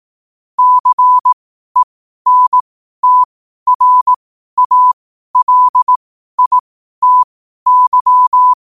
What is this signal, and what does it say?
Morse code, heard as a keyed tone, about 18 wpm, spaced as for 12 wpm, 1000 Hz. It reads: CENTRALITY
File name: qotdfiles / 12wpm